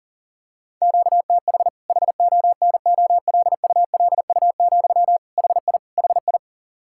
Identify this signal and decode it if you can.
Morse code, heard as a keyed tone, about 40 wpm, 700 Hz. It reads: QTH HONOLULU, HI HI
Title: Practice QSO - Exchange 1 - Beginning - 40wpm